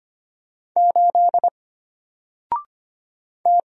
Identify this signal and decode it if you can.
Morse code, heard as a keyed tone, about 25 wpm, 700 Hz. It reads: 8 T
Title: Single Letter-Number Alt 25wpm